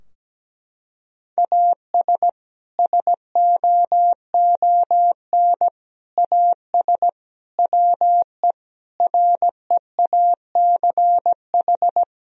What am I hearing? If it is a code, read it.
Morse code, heard as a keyed tone, about 17 wpm, 700 Hz. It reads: AS SOON AS WE REACH
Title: Sentences from Top 1000 Words - Review Part 2 - 17wpm